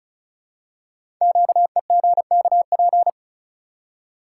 Morse code, heard as a keyed tone, about 36 wpm, 690 Hz. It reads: QEGKP
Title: Code Groups - 5 Letters - 35wpm